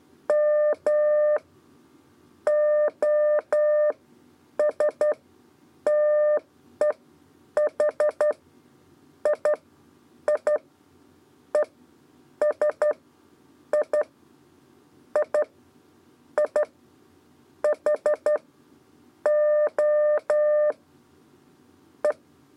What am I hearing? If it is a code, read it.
Morse code, heard as a keyed tone, about 11 wpm, spaced as for 8 wpm, 600 Hz. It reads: M OS TEH I I E S I I I H O E
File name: Lezione trasmessa 2 lez 2 exe